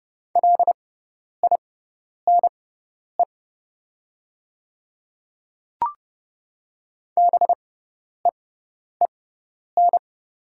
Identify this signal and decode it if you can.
Morse code, heard as a keyed tone, about 30 wpm, spaced as for 10 wpm, 700 Hz. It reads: LINE BEEN
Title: 6x Character Spacing - Top 300 Words - 30wpm